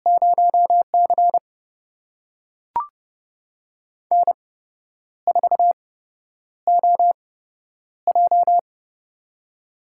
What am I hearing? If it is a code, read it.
Morse code, heard as a keyed tone, about 31 wpm, spaced as for 10 wpm, 700 Hz. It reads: 0C N4OJ
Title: USA Call Signs - 4 Characters - 8x Character Spacing - 30wpm